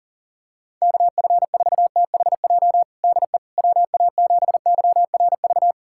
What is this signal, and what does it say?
Morse code, heard as a keyed tone, about 40 wpm, 700 Hz. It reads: KF4THJ DE WA7YRV
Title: USA Callsign DE Callsign 40wpm